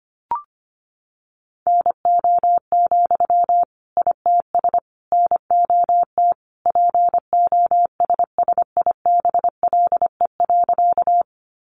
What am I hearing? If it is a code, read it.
Morse code, heard as a keyed tone, about 25 wpm, 700 Hz. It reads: NO, ITS NOT POSSIBLE.
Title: Sentences from Top 600 Words 25wpm